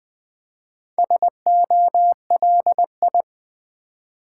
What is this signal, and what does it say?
Morse code, heard as a keyed tone, about 20 wpm, 700 Hz. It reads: SOLI
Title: Build A Word - 4 Letters At A Time - 10-Letter Words - 20wpm